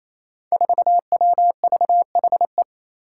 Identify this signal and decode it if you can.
Morse code, heard as a keyed tone, about 28 wpm, 700 Hz. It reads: 4WVHE
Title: Five Challenging Letter-Number Combinations 28wpm